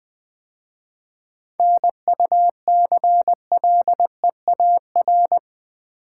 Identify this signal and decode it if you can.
Morse code, heard as a keyed tone, about 20 wpm, 700 Hz. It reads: NUCLEAR
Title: 7 Letter Words - First 3 Letters Given - 20wpm